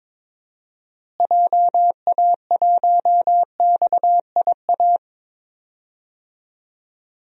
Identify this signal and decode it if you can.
Morse code, heard as a keyed tone, about 21 wpm, 700 Hz. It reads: JA1XIA
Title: DXCC-100 Call Signs - 6 Characters - Repeated 2x - 22wpm